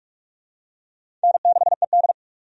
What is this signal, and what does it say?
Morse code, heard as a keyed tone, about 42 wpm, 700 Hz. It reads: N6ED
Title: USA Call Signs - 4 Characters - 45wpm